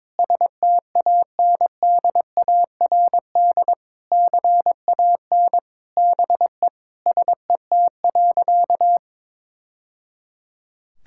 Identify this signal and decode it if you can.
Morse code, heard as a keyed tone, about 22 wpm, 700 Hz. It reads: STANDARD CAN BE SET.
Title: Sentences from Top 1000 Words - Review Part 6 - 22wpm